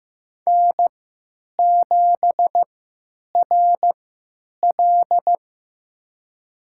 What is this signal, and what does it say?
Morse code, heard as a keyed tone, about 15 wpm, spaced as for 9 wpm, 700 Hz. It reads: N7RL
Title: USA Call Signs - 4 Characters - 3x Character Spacing - 15wpm